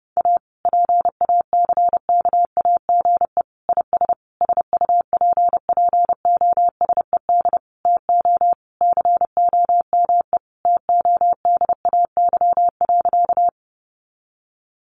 Morse code, heard as a keyed tone, about 30 wpm, 700 Hz. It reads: A PACKAGE IS SUPPOSED TO COME TODAY.
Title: Additional Easy Phrases 30wpm